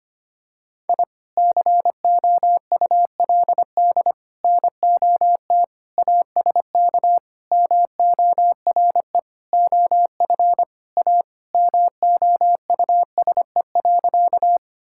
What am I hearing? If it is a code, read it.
Morse code, heard as a keyed tone, about 25 wpm, 700 Hz. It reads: I COULD NOT ASK MORE OF A MOUSE.
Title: Sentences from Top 1000 Words - Review Part 5 - 25wpm